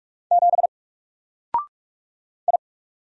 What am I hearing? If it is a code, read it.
Morse code, heard as a keyed tone, about 44 wpm, 700 Hz. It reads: 7 I
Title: Single Letter-Number Alt 45wpm